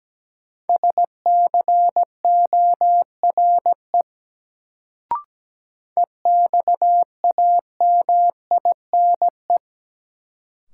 Morse code, heard as a keyed tone, about 17 wpm, 700 Hz. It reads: SCORE EXAMINE
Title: Sentences from Top 1600 Words - New Word Drill - 17wpm